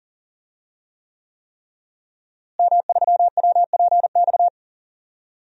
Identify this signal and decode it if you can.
Morse code, heard as a keyed tone, about 39 wpm, 700 Hz. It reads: M3WPX
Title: DXCC-100 Call Signs - 5 Characters - Repeated 3x - 40wpm